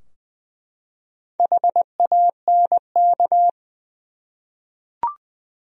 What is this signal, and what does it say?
Morse code, heard as a keyed tone, about 20 wpm, 700 Hz. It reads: HANK
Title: USA Common Names - Max 5 Letters - 20wpm